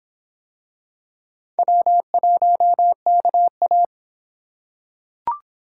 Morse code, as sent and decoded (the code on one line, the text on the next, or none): .-- .---- -.- .-
W1KA